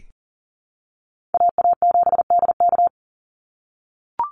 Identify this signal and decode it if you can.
Morse code, heard as a keyed tone, about 40 wpm, 700 Hz. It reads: AA7DK
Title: NA SOTA Chaser Callsigns 40wpm